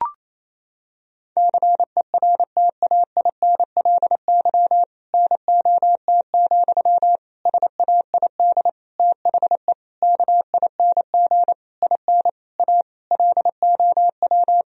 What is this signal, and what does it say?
Morse code, heard as a keyed tone, about 28 wpm, 700 Hz. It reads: CERTAINLY NOT, SAID THE KING IN A LOW
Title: Sentences from Top 1000 Words - Review Part 1 - 28wpm